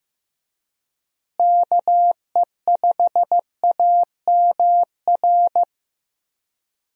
Morse code, heard as a keyed tone, about 15 wpm, 700 Hz. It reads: KE5AMR
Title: USA Call Signs - 6 Characters - Repeated 3x - 15wpm